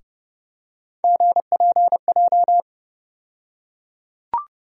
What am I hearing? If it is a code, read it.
Morse code, heard as a keyed tone, about 30 wpm, 700 Hz. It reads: GPJ
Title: USA Call Sign Postfixes - 3 Letters 30wpm